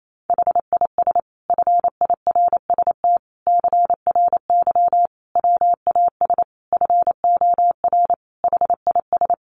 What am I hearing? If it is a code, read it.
Morse code, heard as a keyed tone, about 28 wpm, 700 Hz. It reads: HIS FIRST CRY WAS FOR HIS
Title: Sentences from Top 500 Words 28wpm